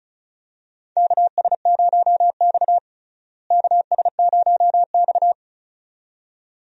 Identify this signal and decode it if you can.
Morse code, heard as a keyed tone, about 34 wpm, 700 Hz. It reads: KS0X KS0X
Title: USA Call Signs - 4 Characters - Repeated 2x - 35wpm